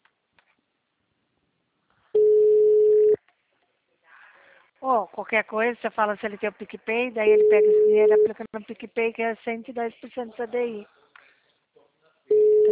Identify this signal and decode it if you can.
Ringback tone (Europe, 425 Hz), with speech.
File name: out- -2020- - - .81716